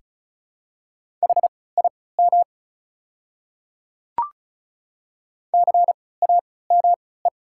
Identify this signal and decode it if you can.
Morse code, heard as a keyed tone, about 36 wpm, spaced as for 20 wpm, 700 Hz. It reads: HIM CAME
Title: 3x Character Spacing - Top 200 Words - 35wpm